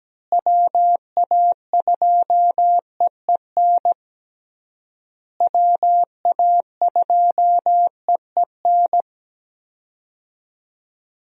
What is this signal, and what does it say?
Morse code, heard as a keyed tone, about 17 wpm, 700 Hz. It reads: WA2EEN WA2EEN
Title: USA Call Signs - 6 Characters - Repeated 3x - 17wpm